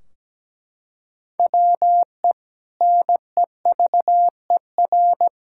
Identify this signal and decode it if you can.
Morse code, heard as a keyed tone, about 17 wpm, 700 Hz. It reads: WE NEVER
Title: Sentences from Top 1600 Words 17wpm